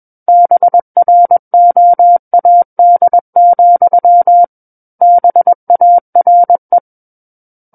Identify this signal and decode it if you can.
Morse code, heard as a keyed tone, about 21 wpm, 700 Hz. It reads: BROAD, BARE
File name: Book One / 12-Chapter 12 - 21 WPM